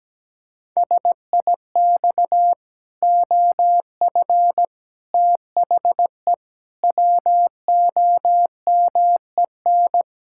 Morse code, heard as a keyed tone, about 17 wpm, 700 Hz. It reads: SIX OF THE WOMEN
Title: Pangram Challenge v2 - 17wpm